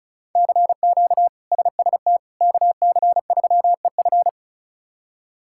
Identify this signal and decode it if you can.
Morse code, heard as a keyed tone, about 36 wpm, 700 Hz. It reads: CQ SST KC3EF
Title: K1USN SST QSO Practice - 35wpm